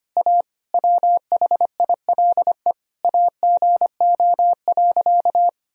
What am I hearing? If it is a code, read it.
Morse code, heard as a keyed tone, about 25 wpm, 700 Hz. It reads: A WHILE AGO.
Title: Sentences from Top 1000 Words - Review Part 1 - 25wpm